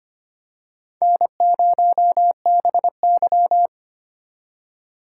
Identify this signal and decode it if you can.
Morse code, heard as a keyed tone, about 25 wpm, 700 Hz. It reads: N0BY
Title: USA Call Signs - 4 Characters - 2x Character Spacing - 25wpm